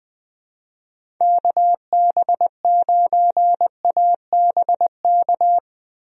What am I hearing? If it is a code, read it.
Morse code, heard as a keyed tone, about 20 wpm, 700 Hz. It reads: KB9ABK
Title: USA Call Signs - 6 Characters - Repeated 3x - 20wpm